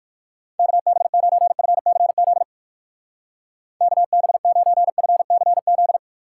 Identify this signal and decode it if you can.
Morse code, heard as a keyed tone, about 57 wpm, 700 Hz. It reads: KB9FCZ KB9FCZ